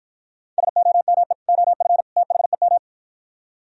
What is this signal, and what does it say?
Morse code, heard as a keyed tone, about 52 wpm, 700 Hz. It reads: SOME OF THEM